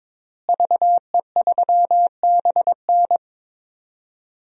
Morse code, heard as a keyed tone, about 23 wpm, 700 Hz. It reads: VE3BN
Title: DXCC-100 Call Signs - 5 Characters - Repeated 2x - 22wpm